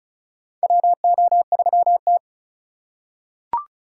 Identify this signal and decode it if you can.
Morse code, heard as a keyed tone, about 36 wpm, 700 Hz. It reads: WO3T
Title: USA Call Signs - 4 Characters - 3x Character Spacing - 35wpm